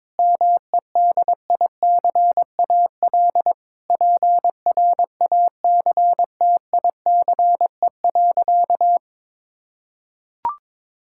Morse code, heard as a keyed tone, about 22 wpm, 700 Hz. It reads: MEDICAL PRACTICE.
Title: Sentences from Top 1000 Words - Review Part 5 - 22wpm